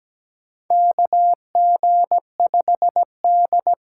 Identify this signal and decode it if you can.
Morse code, heard as a keyed tone, about 17 wpm, 700 Hz. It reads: KG5D
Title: USA Call Signs - 6 Characters - 8x Character Spacing - 17wpm